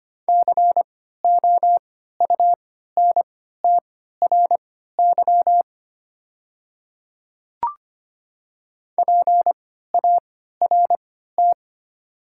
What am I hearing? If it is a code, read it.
Morse code, heard as a keyed tone, about 25 wpm, spaced as for 14 wpm, 700 Hz. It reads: COUNTRY PART
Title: 3x Character Spacing - Top 300 Words - 25wpm